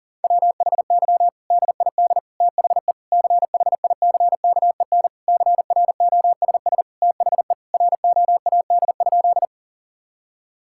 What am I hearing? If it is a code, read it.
Morse code, heard as a keyed tone, about 37 wpm, 700 Hz. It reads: WHY DID THE CHICKEN CROSS THE ROAD?
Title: Sentences from Top 1000 Words - Review Part 5 - 40wpm